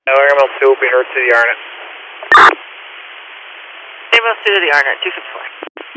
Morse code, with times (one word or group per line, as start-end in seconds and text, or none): none